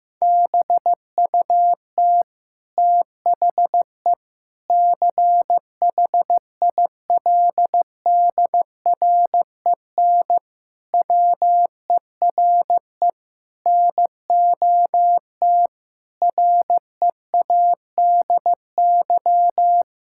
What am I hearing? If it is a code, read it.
Morse code, heard as a keyed tone, about 15 wpm, 700 Hz. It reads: BUT THE CHILDREN WERE NOT READY
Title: Sentences from Top 1000 Words - Review Part 6 - 15wpm